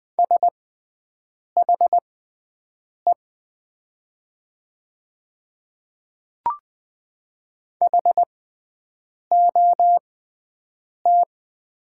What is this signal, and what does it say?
Morse code, heard as a keyed tone, about 20 wpm, spaced as for 7 wpm, 700 Hz. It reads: SHE HOT
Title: 6x Character Spacing - Top 200 Words - 20wpm